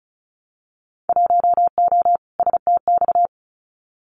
Morse code, heard as a keyed tone, about 36 wpm, 700 Hz. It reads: 1O STX
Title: Winter Field Day Exchanges 35wpm